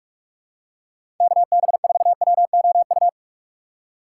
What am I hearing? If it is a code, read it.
Morse code, heard as a keyed tone, about 46 wpm, 700 Hz. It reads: KB4WOU
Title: USA Call Signs - 4x Character Spacing - 45wpm